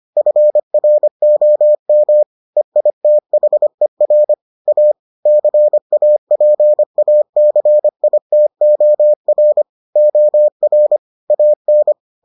Morse code, heard as a keyed tone, about 25 wpm, 600 Hz. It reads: FROM EITHER A CAPACITOR OR AN